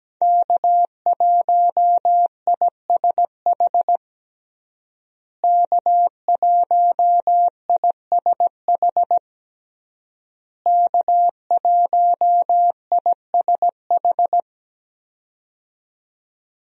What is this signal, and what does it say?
Morse code, heard as a keyed tone, about 17 wpm, 700 Hz. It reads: K1ISH K1ISH K1ISH
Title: USA Call Signs - 5 Characters - Repeated 3x - 17wpm